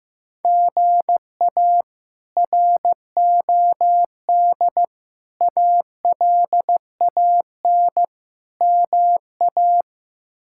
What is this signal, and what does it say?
Morse code, heard as a keyed tone, about 16 wpm, 700 Hz. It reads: GA ROD ALAN MA
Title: K1USN SST QSO Practice - 15wpm